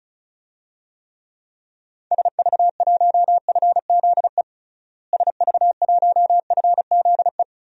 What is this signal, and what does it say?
Morse code, heard as a keyed tone, about 36 wpm, 700 Hz. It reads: SV1FZE SV1FZE
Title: DXCC-100 Call Signs - 6 Characters - Repeated 3x - 35wpm